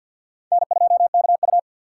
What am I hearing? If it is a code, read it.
Morse code, heard as a keyed tone, about 61 wpm, 700 Hz. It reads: N2KU
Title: Speed Racing - USA 1x2 Call Signs 50wpm